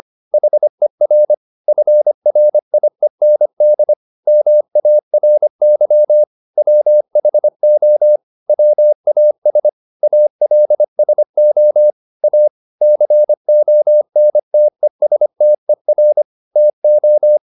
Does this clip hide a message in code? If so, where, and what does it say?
Yes: Morse code from 0.3 to 17.4 s, heard as a keyed tone, about 25 wpm, 600 Hz. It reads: HER FRIEND MARY WHO WAS ALSO A CONTESTER TO